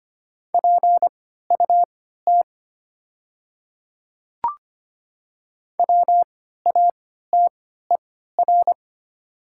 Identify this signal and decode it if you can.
Morse code, heard as a keyed tone, about 25 wpm, spaced as for 14 wpm, 700 Hz. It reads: PUT WATER
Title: 3x Character Spacing - Top 200 Words - 25wpm